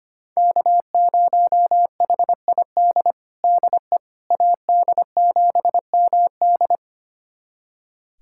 Morse code, heard as a keyed tone, about 25 wpm, 700 Hz. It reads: K0HID DE AD7MD
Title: USA Callsign DE Callsign - 5 Characters - 25wpm